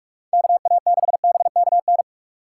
Morse code, heard as a keyed tone, about 46 wpm, 690 Hz. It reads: KA6BKN
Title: USA Call Signs - 6 Characters - 3x Character Spacing - 45wpm